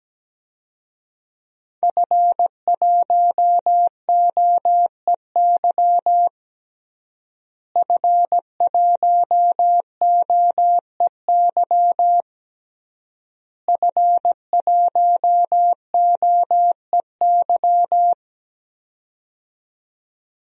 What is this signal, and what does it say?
Morse code, heard as a keyed tone, about 17 wpm, 700 Hz. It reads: F1OEY F1OEY F1OEY
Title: DXCC-100 Call Signs - 5 Characters - Repeated 3x - 17wpm